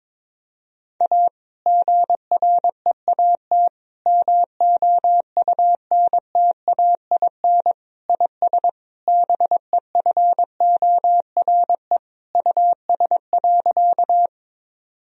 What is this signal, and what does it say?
Morse code, heard as a keyed tone, about 21 wpm, 700 Hz. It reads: A GREAT MOUNTAIN IS BEFORE US.
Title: Sentences from Top 1000 Words - Review Part 1 - 22wpm